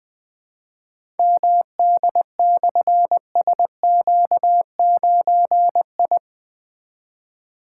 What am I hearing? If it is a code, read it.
Morse code, heard as a keyed tone, about 20 wpm, 700 Hz. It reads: MD/SQ9I
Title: Difficult Call Signs 20wpm